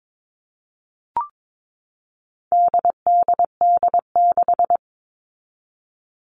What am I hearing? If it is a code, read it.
Morse code, heard as a keyed tone, about 22 wpm, 700 Hz. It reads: DDD6
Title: All Cut Numbers - 4-Digit 22wpm